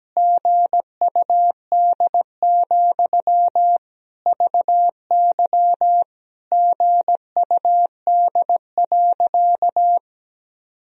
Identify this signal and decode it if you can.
Morse code, heard as a keyed tone, about 17 wpm, 700 Hz. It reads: GUD, VY GUD.
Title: Sentences from Top 200 Words and Abbreviations 17wpm